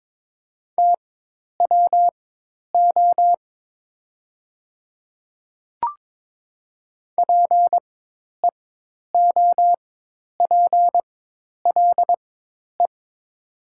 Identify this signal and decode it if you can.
Morse code, heard as a keyed tone, about 22 wpm, spaced as for 10 wpm, 700 Hz. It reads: TWO PEOPLE
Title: 4x Character Spacing - Top 100 Words - 22wpm